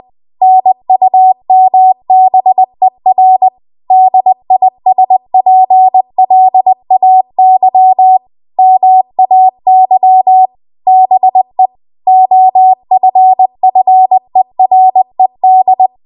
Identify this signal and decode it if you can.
Morse code, heard as a keyed tone, about 20 wpm, 750 Hz. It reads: NUMBER DISPLAY MAY BE OFFERED
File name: Archive / 20 WPM / 20WPM